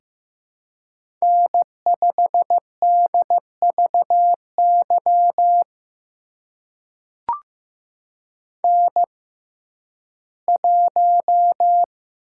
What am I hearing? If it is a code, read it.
Morse code, heard as a keyed tone, about 15 wpm, spaced as for 11 wpm, 700 Hz. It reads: N5DVY N1
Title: USA Call Signs - 5 Characters - 6x Character Spacing - 15wpm